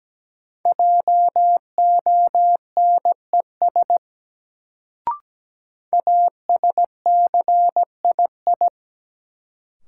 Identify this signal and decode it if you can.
Morse code, heard as a keyed tone, about 17 wpm, 700 Hz. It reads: JONES ASCII
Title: Sets of 5 Words - New Word Drill - 17wpm